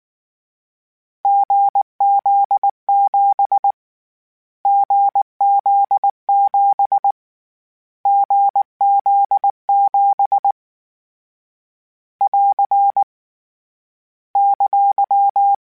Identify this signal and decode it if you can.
Morse code, heard as a keyed tone, about 19 wpm, 800 Hz. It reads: GZ7 GZ7 GZ7 <AR> !
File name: CW Code Groups-mp3s / Sets mp3s / GZ GZ7 sp @18